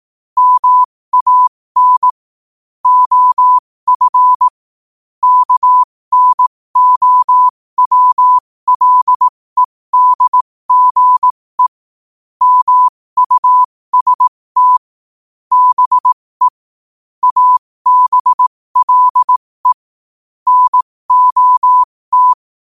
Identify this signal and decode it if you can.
Morse code, heard as a keyed tone, about 18 wpm, spaced as for 15 wpm, 1000 Hz. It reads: MAN OF KNOWLEDGE MUST BE ABLE NOT
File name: qotdfiles / 15wpm